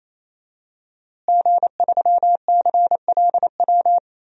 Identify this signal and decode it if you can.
Morse code, heard as a keyed tone, about 28 wpm, 700 Hz. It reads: G3CLW
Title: DXCC-100 Call Signs - 5 Characters - Repeated 3x - 28wpm